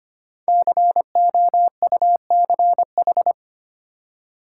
Morse code, heard as a keyed tone, about 25 wpm, 690 Hz. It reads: COUCH
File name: practice / 25wpm / Easy Words with Hints - 25wpm